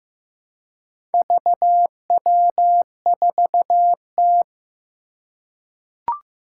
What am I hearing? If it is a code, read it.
Morse code, heard as a keyed tone, about 15 wpm, 700 Hz. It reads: VW4T
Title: Four Challenging Letter-Number Combinations 15wpm